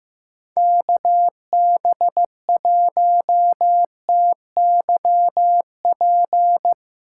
Morse code, heard as a keyed tone, about 15 wpm, 700 Hz. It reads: KB1TYP
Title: Speed Racing - USA 2x3 Call Signs 15wpm